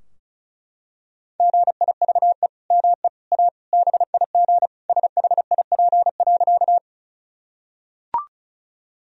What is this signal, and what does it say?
Morse code, heard as a keyed tone, about 34 wpm, 700 Hz. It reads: GIVE ME A BIG SHIP.
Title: Sentences from Top 1000 Words - Review Part 2 - 35wpm